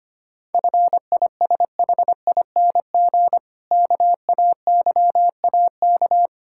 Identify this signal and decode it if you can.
Morse code, heard as a keyed tone, about 25 wpm, 700 Hz. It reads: FISHING KAYAK
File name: practice / 25wpm / Difficult Words - Speed-Racing with Hints - 25wpm